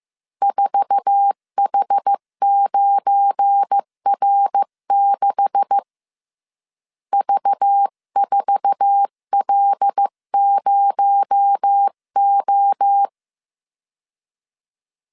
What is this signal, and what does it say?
Morse code, heard as a keyed tone, about 15 wpm, 800 Hz. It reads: 4H9R6 V4L0O
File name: CW / DET / 1 9 1500C S B 14wpm 3302